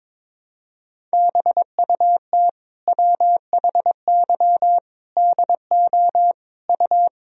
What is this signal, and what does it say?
Morse code, heard as a keyed tone, about 22 wpm, 700 Hz. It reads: BUT WHY DO U
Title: Sentences from Top 200 Words and Abbreviations 22wpm